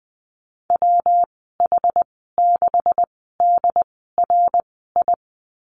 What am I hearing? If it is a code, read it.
Morse code, heard as a keyed tone, about 20 wpm, spaced as for 15 wpm, 700 Hz. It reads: WH6DRI
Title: 128 Lesson 21g - Number 6 - Call Signs 20wpm